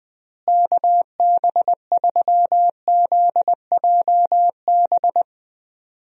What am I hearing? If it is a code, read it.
Morse code, heard as a keyed tone, about 20 wpm, 700 Hz. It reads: KB3ZJB
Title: USA Call Signs - 6 Characters - Repeated 3x - 20wpm